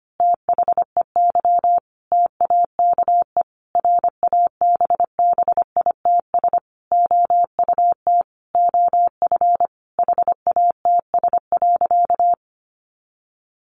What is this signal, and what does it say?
Morse code, heard as a keyed tone, about 25 wpm, 700 Hz. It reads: THEY TAKE RABBITS OUT OF HATS.
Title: Easy Phrases 25wpm